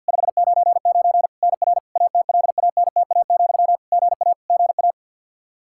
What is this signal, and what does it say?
Morse code, heard as a keyed tone, about 48 wpm, 700 Hz. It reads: 599 NR ATLANTA, GA GA
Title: Practice QSO - Exchange 1 - Beginning - 50wpm